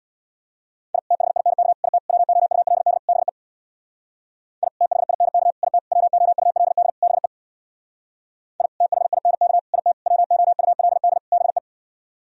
Morse code, heard as a keyed tone, about 65 wpm, 690 Hz. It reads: I THINK IT WOULD BE I THINK IT WOULD BE I THINK IT WOULD BE